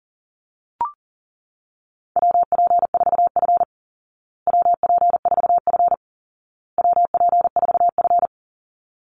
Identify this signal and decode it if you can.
Morse code, heard as a keyed tone, about 40 wpm, 700 Hz. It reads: WP4F WP4F WP4F
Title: Call Signs Repeated 2x - 40wpm